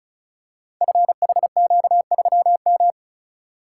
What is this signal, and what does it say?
Morse code, heard as a keyed tone, about 36 wpm, 690 Hz. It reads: FHQ3M
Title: Code Groups - 5 Letter:Numbers - 35wpm